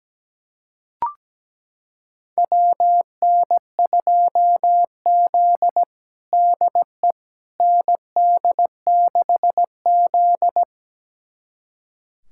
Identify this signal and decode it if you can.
Morse code, heard as a keyed tone, about 17 wpm, 700 Hz. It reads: WN2Z DE ND6Z
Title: USA Callsign DE Callsign - 4 Characters - 17wpm